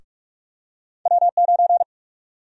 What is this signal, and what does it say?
Morse code, heard as a keyed tone, about 46 wpm, 700 Hz. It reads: W9
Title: USA Call Sign Prefixes - Repeated 2x - 45wpm